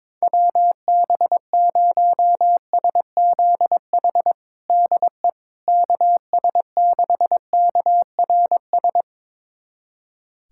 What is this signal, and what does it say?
Morse code, heard as a keyed tone, about 22 wpm, 700 Hz. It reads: WB0SZH DE KS6KRS
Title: USA Callsign DE Callsign 22wpm